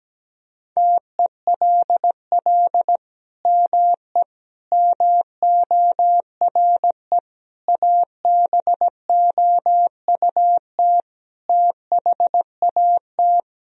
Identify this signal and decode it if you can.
Morse code, heard as a keyed tone, about 17 wpm, 700 Hz. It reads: TELL ME MORE ABOUT THAT
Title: Sentences from Top 1000 Words - Review Part 4 - 17wpm